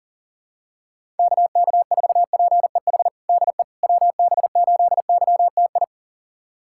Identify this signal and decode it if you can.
Morse code, heard as a keyed tone, about 40 wpm, 700 Hz. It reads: KK4PEH DE WB8YTI
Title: USA Callsign DE Callsign - 6 Characters - 40wpm